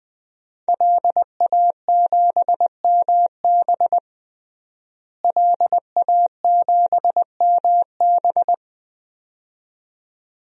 Morse code, heard as a keyed tone, about 20 wpm, 700 Hz. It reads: LA7MB LA7MB
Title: DXCC-100 Call Signs - 5 Characters - Repeated 2x - 20wpm